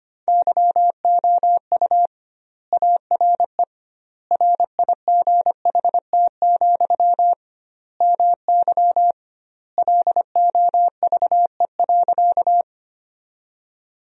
Morse code, heard as a keyed tone, about 25 wpm, 700 Hz. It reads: YOU ARE RIGHT, MY LOVE.
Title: 2x Word-Spacing - Sentences from Top 400 Words - 25wpm